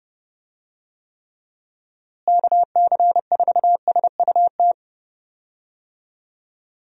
Morse code, heard as a keyed tone, about 30 wpm, 700 Hz. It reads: KC4SUT
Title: USA Call Signs - 6 Characters - Repeated 2x - 30wpm